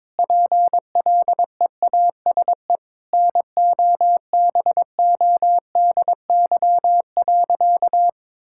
Morse code, heard as a keyed tone, about 22 wpm, 700 Hz. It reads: PLEASE NOBODY.
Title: Sentences from Top 1600 Words 22wpm